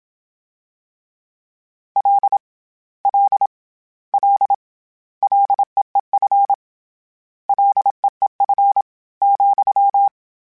Morse code, heard as a keyed tone, about 27 wpm, 800 Hz. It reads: L L L LEEF LEEF ,
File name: Short Runs / Short Runs-mp3s / SinglePrefixes @25